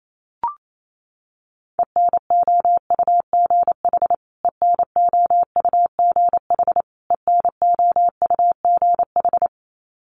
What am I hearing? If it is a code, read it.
Morse code, heard as a keyed tone, about 28 wpm, 700 Hz. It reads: ENOUGH ENOUGH ENOUGH
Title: Top 500 Words Repeated 28wpm